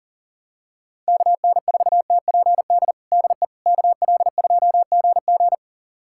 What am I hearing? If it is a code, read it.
Morse code, heard as a keyed tone, about 37 wpm, 700 Hz. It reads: KN4TPD DE KL2GG
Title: USA Callsign DE Callsign 40wpm